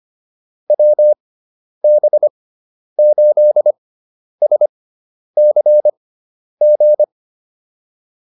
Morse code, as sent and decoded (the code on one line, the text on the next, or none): .-- -... ---.. ... -.-. --.
WB8SCG